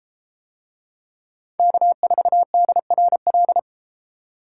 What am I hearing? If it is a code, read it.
Morse code, heard as a keyed tone, about 32 wpm, 700 Hz. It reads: K4DRL